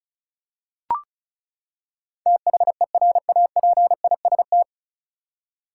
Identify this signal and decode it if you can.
Morse code, heard as a keyed tone, about 34 wpm, 700 Hz. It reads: THERAPIST
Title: Practice QSO - Exchange 3 - More Info - 35wpm